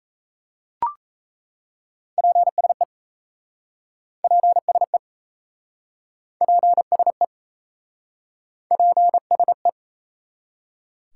Morse code, heard as a keyed tone, about 37 wpm, 700 Hz. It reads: PSE PSE PSE PSE